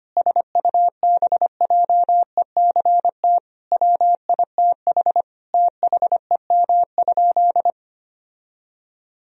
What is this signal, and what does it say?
Morse code, heard as a keyed tone, about 25 wpm, 700 Hz. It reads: SUBJECT WITH THEM?
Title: Sentences from Top 1000 Words - Review Part 1 - 25wpm